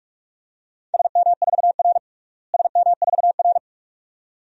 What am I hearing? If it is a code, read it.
Morse code, heard as a keyed tone, about 45 wpm, 700 Hz. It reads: SM4R SM4R
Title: DXCC-100 Call Signs - 4 Characters - Repeated 2x - 45wpm